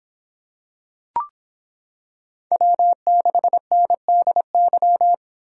Morse code, heard as a keyed tone, about 26 wpm, 700 Hz. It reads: W6NDY